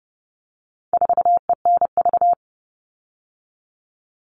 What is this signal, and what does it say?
Morse code, heard as a keyed tone, about 30 wpm, 700 Hz. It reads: 4ENV
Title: All Cut Numbers - 4-Digit 30wpm